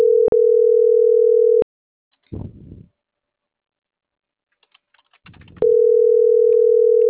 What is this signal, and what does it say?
Ringback tone (North America), 440 + 480 Hz.